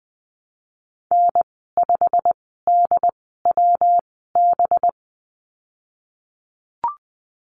Morse code, heard as a keyed tone, about 19 wpm, spaced as for 14 wpm, 700 Hz. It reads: N5DWB
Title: 193 Lesson 31g - Letter v - Call Signs 20wpm